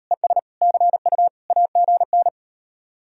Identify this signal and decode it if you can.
Morse code, heard as a keyed tone, about 38 wpm, 700 Hz. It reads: ES CU AGN